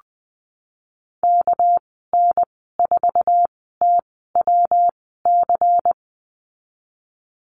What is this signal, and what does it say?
Morse code, heard as a keyed tone, about 20 wpm, spaced as for 15 wpm, 700 Hz. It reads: KN4TWC
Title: 200 Lesson 32g - Letter k - Call Signs 20wpm